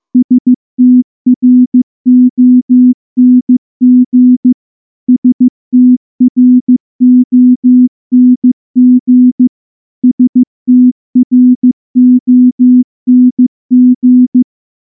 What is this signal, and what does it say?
Morse code, heard as a keyed tone, about 14 wpm, 250 Hz. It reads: STRONG STRONG STRONG
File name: morse-training-15WPM